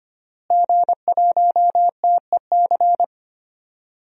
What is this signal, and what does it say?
Morse code, heard as a keyed tone, about 25 wpm, 700 Hz. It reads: G1TEC
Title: DXCC-100 Call Signs - 5 Characters - Repeated 3x - 25wpm